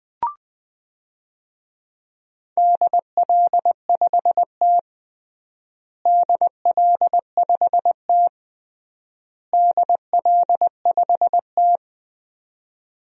Morse code, heard as a keyed tone, about 20 wpm, 700 Hz. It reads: DL5T DL5T DL5T
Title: DXCC-100 Call Signs - 4 Characters - Repeated 3x - 20wpm